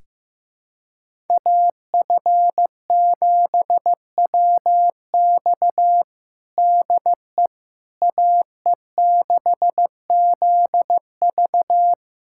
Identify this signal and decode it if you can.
Morse code, heard as a keyed tone, about 15 wpm, 700 Hz. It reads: AF7WX DE AE6ZV
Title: USA Callsign DE Callsign - 5 Characters - 15wpm